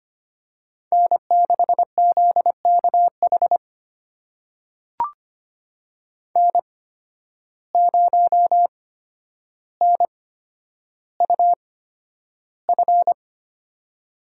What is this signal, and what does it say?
Morse code, heard as a keyed tone, about 25 wpm, spaced as for 11 wpm, 700 Hz. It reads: N6ZKH N0NUF
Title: USA Call Signs - 5 Characters - 8x Character Spacing - 25wpm